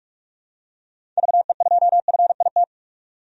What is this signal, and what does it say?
Morse code, heard as a keyed tone, about 46 wpm, 700 Hz. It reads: VE2FIT
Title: DXCC-100 Call Signs - 6 Characters - 45wpm